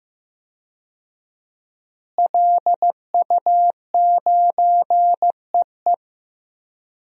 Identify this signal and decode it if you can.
Morse code, heard as a keyed tone, about 16 wpm, 700 Hz. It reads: LU9EE
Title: DXCC-100 Call Signs - 5 Characters - Repeated 2x - 15wpm